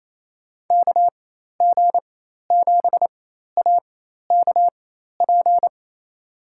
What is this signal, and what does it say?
Morse code, heard as a keyed tone, about 28 wpm, spaced as for 13 wpm, 700 Hz. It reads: KG7AKP
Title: USA Call Signs - 4x Character Spacing - 28wpm